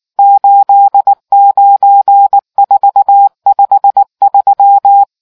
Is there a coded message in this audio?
Yes: Morse code from 0.2 to 5.0 s, heard as a keyed tone, about 19 wpm, 800 Hz. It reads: 89453